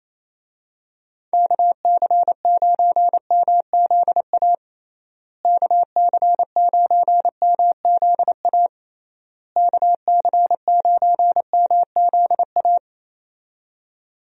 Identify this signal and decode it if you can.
Morse code, heard as a keyed tone, about 28 wpm, 700 Hz. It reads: KC9MZA KC9MZA KC9MZA
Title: USA Call Signs - 6 Characters - Repeated 3x - 28wpm